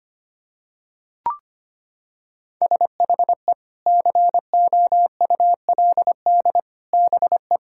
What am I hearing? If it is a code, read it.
Morse code, heard as a keyed tone, about 25 wpm, 700 Hz. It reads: SHE COULD BE
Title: Sentences from Top 1600 Words 25wpm